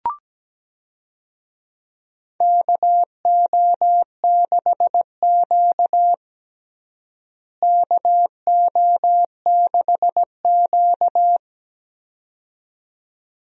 Morse code, heard as a keyed tone, about 17 wpm, 700 Hz. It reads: KO6Q KO6Q
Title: USA Call Signs - 4 Characters - Repeated 2x - 17wpm